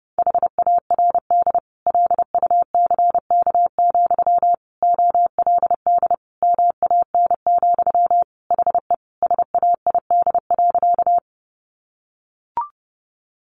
Morse code, heard as a keyed tone, about 30 wpm, 700 Hz. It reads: HARD LUCK, OLD MAN, HE SAID.
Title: Sentences from Top 1500 Words 30wpm